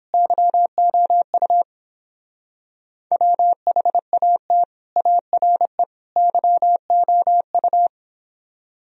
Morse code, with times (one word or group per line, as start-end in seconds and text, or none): 0.1-1.6 YOU
3.1-4.6 WHAT
5.0-5.8 ARE
6.2-7.9 YOU